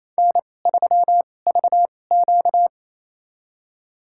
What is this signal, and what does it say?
Morse code, heard as a keyed tone, about 28 wpm, spaced as for 20 wpm, 700 Hz. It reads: N3VQ
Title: USA Call Signs - 4 Characters - 2x Character Spacing - 28wpm